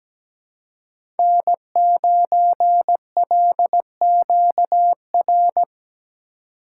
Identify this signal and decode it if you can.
Morse code, heard as a keyed tone, about 17 wpm, 700 Hz. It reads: N9LQR
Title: USA Call Signs - 5 Characters - 3x Character Spacing - 17wpm